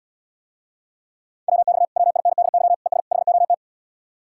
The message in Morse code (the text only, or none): WX WINDY ES HOT